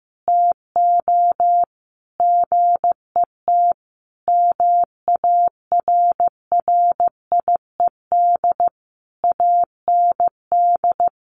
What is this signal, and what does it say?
Morse code, heard as a keyed tone, about 15 wpm, 700 Hz. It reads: TO GET MARRIED AND
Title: Easy Phrases 15wpm